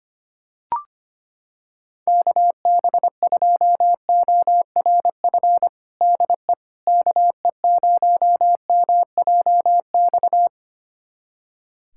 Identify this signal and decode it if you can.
Morse code, heard as a keyed tone, about 25 wpm, 700 Hz. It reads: KB2ORF DE KE0MJX
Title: USA Callsign DE Callsign - 6 Characters - 25wpm